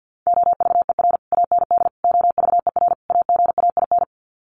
Morse code, heard as a keyed tone, about 50 wpm, 700 Hz. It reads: OVER AND OVER AGAIN
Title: Sets of 4 Words 50wpm